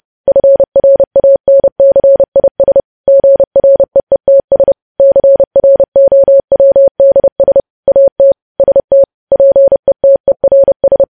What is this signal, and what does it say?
Morse code, heard as a keyed tone, about 30 wpm, 550 Hz. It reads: FRANCIS GREETS CROWDS AT ST PETERS
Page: news headlines in morse code at 30 wpm